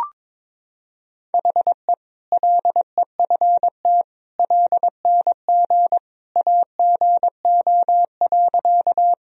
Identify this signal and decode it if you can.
Morse code, heard as a keyed tone, about 22 wpm, 700 Hz. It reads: HE LEFT LNG AGO.
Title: Sentences from Top 500 Words and Abbreviations 22wpm